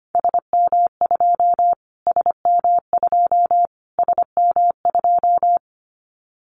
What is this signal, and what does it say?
Morse code, heard as a keyed tone, about 25 wpm, 700 Hz. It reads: SM2 SM2 SM2
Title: Callsign Prefixes for Top 25 DXCC - Repeated 3x - 25wpm